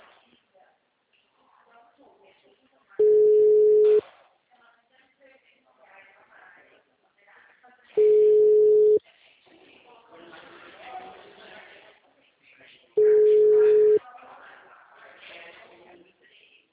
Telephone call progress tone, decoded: ringback tone

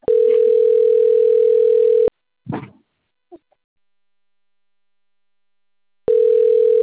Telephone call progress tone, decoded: ringback tone

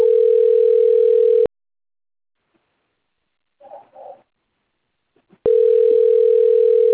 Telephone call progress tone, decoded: ringback tone

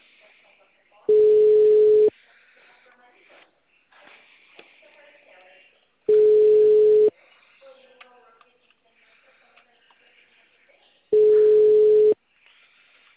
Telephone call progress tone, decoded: ringback tone

